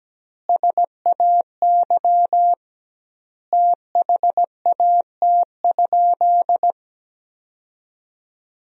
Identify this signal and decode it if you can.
Morse code, heard as a keyed tone, about 17 wpm, 700 Hz. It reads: SAY THAT?
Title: 2x Word-Spacing - Sentences from Top 200 Words - 17wpm